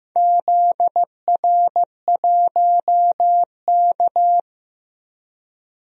Morse code, heard as a keyed tone, about 15 wpm, 700 Hz. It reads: ZR1K
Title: DXCC-100 Call Signs - 4 Characters - Repeated 2x - 15wpm